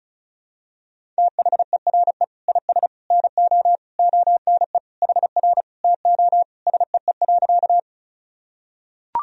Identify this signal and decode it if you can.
Morse code, heard as a keyed tone, about 34 wpm, 700 Hz. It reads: THERE IS NO ONE HR TO SEE.
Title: Sentences from Top 200 Words and Abbreviations 35wpm